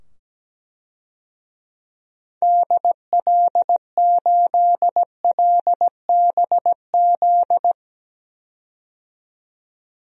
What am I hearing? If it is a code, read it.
Morse code, heard as a keyed tone, about 17 wpm, 700 Hz. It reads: DL8LBZ
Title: DXCC-100 Call Signs - 6 Characters - 17wpm